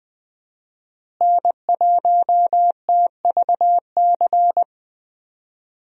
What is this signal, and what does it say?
Morse code, heard as a keyed tone, about 20 wpm, 700 Hz. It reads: N1TVC
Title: USA Call Signs - 5 Characters - Repeated 2x - 20wpm